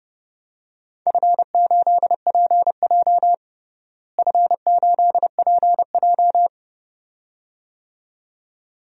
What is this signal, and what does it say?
Morse code, heard as a keyed tone, about 32 wpm, 700 Hz. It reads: F8PJ F8PJ
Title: DXCC-100 Call Signs - 4 Characters - Repeated 2x - 30wpm